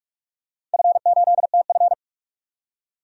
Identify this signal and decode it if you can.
Morse code, heard as a keyed tone, about 42 wpm, 700 Hz. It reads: F8TF
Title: DXCC-100 Call Signs - 4 Characters - 45wpm